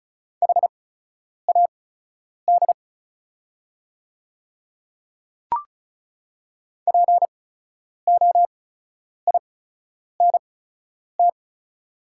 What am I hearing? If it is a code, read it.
Morse code, heard as a keyed tone, about 36 wpm, spaced as for 10 wpm, 700 Hz. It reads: HAD POINT
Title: 8x Character Spacing - Top 200 Words - 35wpm